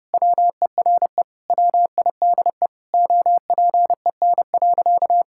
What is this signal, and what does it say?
Morse code, heard as a keyed tone, about 30 wpm, 700 Hz. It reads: WERE WIDE OPEN.
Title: Sentences from Top 1000 Words - Review Part 6 - 30wpm